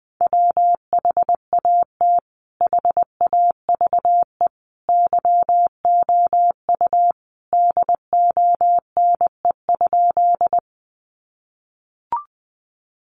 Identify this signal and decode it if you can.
Morse code, heard as a keyed tone, about 19 wpm, 700 Hz. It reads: WHAT HAVE YOU DONE?
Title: Sentences from Top 500 Words 20wpm